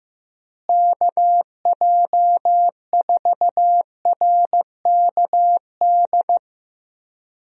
Morse code, heard as a keyed tone, about 16 wpm, 700 Hz. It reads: KJ4RKD
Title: USA Call Signs - 6 Characters - Repeated 3x - 15wpm